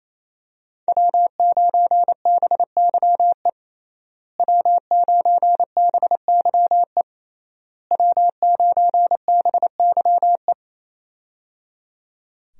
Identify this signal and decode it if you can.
Morse code, heard as a keyed tone, about 28 wpm, 700 Hz. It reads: W9BYE W9BYE W9BYE
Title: USA Call Signs - 5 Characters - Repeated 3x - 28wpm